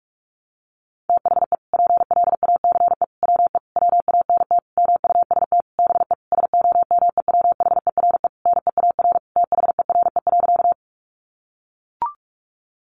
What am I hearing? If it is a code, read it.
Morse code, heard as a keyed tone, about 44 wpm, 700 Hz. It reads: THE PLACE WE WANT MUST BE SOMEWHERE NEAR THERE.
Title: Sentences from Top 1400 Words 45wpm